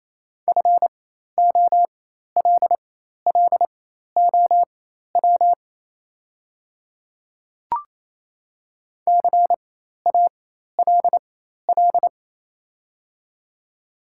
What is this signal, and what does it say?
Morse code, heard as a keyed tone, about 28 wpm, spaced as for 13 wpm, 700 Hz. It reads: FOLLOW CALL
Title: 4x Character Spacing - Top 500 Words - 28wpm